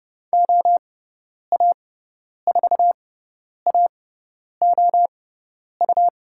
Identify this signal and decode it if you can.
Morse code, heard as a keyed tone, about 30 wpm, spaced as for 10 wpm, 700 Hz. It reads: OA4AOU
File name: Lesson 18g - Letter m - Call Signs 30wpm